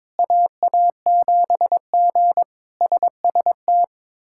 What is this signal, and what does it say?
Morse code, heard as a keyed tone, about 22 wpm, 700 Hz. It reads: AA7G SST
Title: K1USN SST QSO Practice - 22wpm